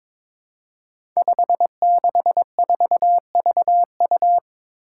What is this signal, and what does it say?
Morse code, heard as a keyed tone, about 22 wpm, 700 Hz. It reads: 564VU
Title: Five Challenging Letter-Number Combinations 22wpm